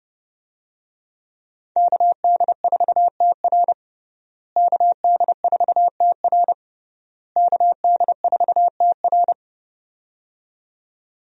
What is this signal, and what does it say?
Morse code, heard as a keyed tone, about 29 wpm, 700 Hz. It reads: KD4TR KD4TR KD4TR
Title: USA Call Signs - 5 Characters - Repeated 3x - 30wpm